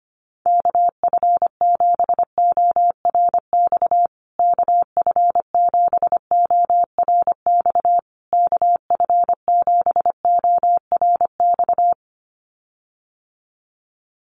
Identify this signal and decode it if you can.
Morse code, heard as a keyed tone, about 25 wpm, 700 Hz. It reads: KF7ORX KF7ORX KF7ORX
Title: USA Call Signs Repeated 3x 25wpm